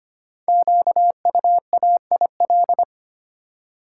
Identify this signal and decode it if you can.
Morse code, heard as a keyed tone, about 25 wpm, 700 Hz. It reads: QUAIL
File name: practice / 25wpm / Difficult Words with Hints - 25wpm